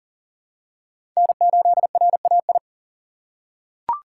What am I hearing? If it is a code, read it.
Morse code, heard as a keyed tone, about 40 wpm, 700 Hz. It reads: N8RAI
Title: USA Call Signs - 5 Characters - 3x Character Spacing - 40wpm